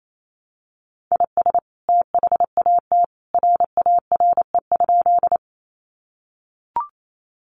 Morse code, heard as a keyed tone, about 28 wpm, 700 Hz. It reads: IS THAT RARE?
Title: Sentences from Top 1100 Words 28wpm